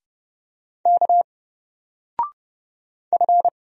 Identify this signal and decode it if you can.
Morse code, heard as a keyed tone, about 30 wpm, 700 Hz. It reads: K F
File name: Single-Letter-Number-30wpm